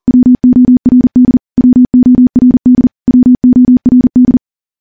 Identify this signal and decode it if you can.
Morse code, heard as a keyed tone, about 40 wpm, 250 Hz. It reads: WORD WORD WORD
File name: morse / morse-training-40WPM